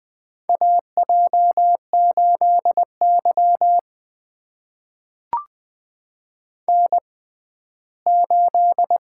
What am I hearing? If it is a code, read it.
Morse code, heard as a keyed tone, about 20 wpm, spaced as for 14 wpm, 700 Hz. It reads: AJ8Y N8
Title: USA Call Signs - 6x Character Spacing - 20wpm